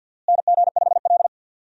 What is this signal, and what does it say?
Morse code, heard as a keyed tone, about 48 wpm, 700 Hz. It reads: NG5L
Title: USA Call Signs - 4 Characters - Repeated 2x - 50wpm